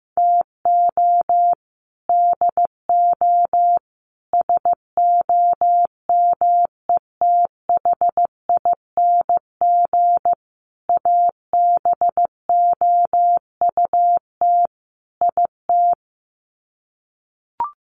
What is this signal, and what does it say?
Morse code, heard as a keyed tone, about 15 wpm, 700 Hz. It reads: TO DO SOMETHING ABOUT IT
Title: Sets of 5 Words 15wpm